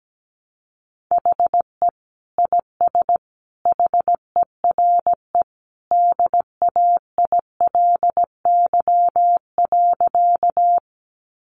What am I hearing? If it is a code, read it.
Morse code, heard as a keyed tone, about 17 wpm, 700 Hz. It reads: HE IS HERE DAILY.
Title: Sentences from Top 1100 Words 17wpm